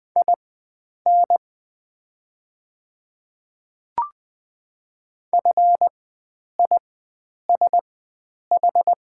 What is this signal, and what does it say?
Morse code, heard as a keyed tone, about 20 wpm, spaced as for 9 wpm, 690 Hz. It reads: IN FISH
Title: 4x Character Spacing - Top 300 Words - 20wpm